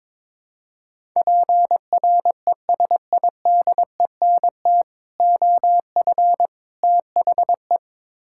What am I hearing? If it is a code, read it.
Morse code, heard as a keyed tone, about 22 wpm, 700 Hz. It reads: PRESIDENT OF THE
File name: practice / 22wpm / Build A Phrase - 5-Word Phrases - 22wpm